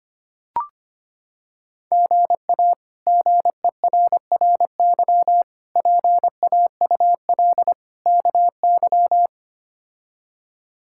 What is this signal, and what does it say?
Morse code, heard as a keyed tone, about 25 wpm, 700 Hz. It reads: GA GERRY PAUL KY
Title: K1USN SST QSO Practice - 25wpm